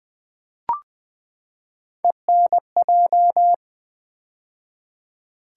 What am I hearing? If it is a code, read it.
Morse code, heard as a keyed tone, about 20 wpm, 700 Hz. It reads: ENJ
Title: Build A Word - 3 Letters At A Time - 9-Letter Words - 20wpm